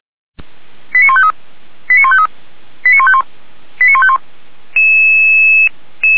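Five-tone selective call (ZVEI), heard as quick runs of five tones, tones of about 70 ms. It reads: 89153 89153 89151 89151
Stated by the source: ZVEI five-tone selective call received by SDR.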